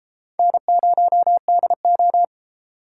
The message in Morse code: N0DO

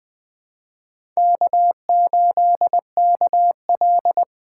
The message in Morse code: K8KL

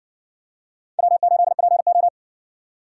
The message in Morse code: W8PO

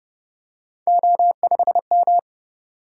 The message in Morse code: O5M

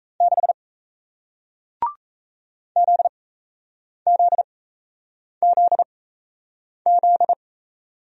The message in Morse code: 6 ZZZZ

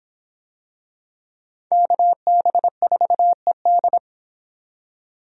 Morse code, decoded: KB4ED

